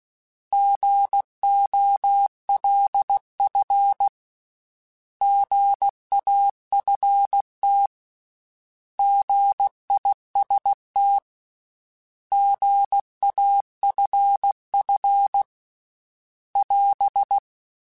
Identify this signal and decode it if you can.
Morse code, heard as a keyed tone, about 16 wpm, 800 Hz. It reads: GOLF GAFT GIST GAFF <AS>